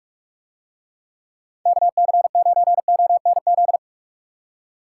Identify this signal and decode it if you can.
Morse code, heard as a keyed tone, about 49 wpm, 700 Hz. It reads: KC9ONZ